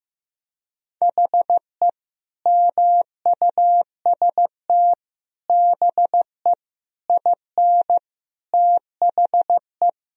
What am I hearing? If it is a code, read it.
Morse code, heard as a keyed tone, about 15 wpm, 700 Hz. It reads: HE MUST BE IN THE
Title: Sentences from Top 1000 Words - Review Part 2 - 15wpm